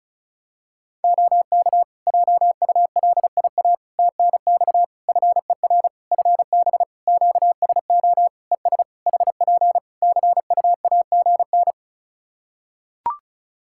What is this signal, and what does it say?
Morse code, heard as a keyed tone, about 34 wpm, 700 Hz. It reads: OK JULIA TNX FER FB QSO ES HP CUAGN
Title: Practice QSO - Exchange 4 - Ending - 35wpm